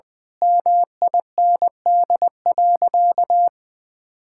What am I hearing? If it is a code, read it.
Morse code, heard as a keyed tone, about 19 wpm, 700 Hz. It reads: MIND.
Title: Difficult Sentences with Hints - 20wpm